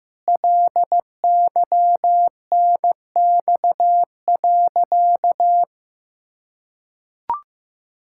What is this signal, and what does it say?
Morse code, heard as a keyed tone, about 16 wpm, 700 Hz. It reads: LYNX.
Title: Pangram Challenge v2 - 15wpm